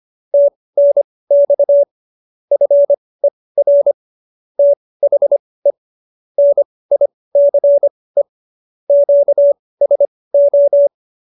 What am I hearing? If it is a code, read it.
Morse code, heard as a keyed tone, about 25 wpm, spaced as for 18 wpm, 570 Hz. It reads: TNX FER THE NICE QSO